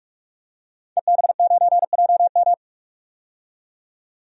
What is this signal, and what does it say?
Morse code, heard as a keyed tone, about 44 wpm, 700 Hz. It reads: EB9JM
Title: DXCC-100 Call Signs - 5 Characters - 45wpm